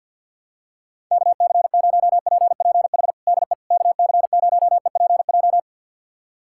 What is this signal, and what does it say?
Morse code, heard as a keyed tone, about 52 wpm, 700 Hz. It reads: KC0PPH DE KC0EPJ
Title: USA Callsign DE Callsign 50wpm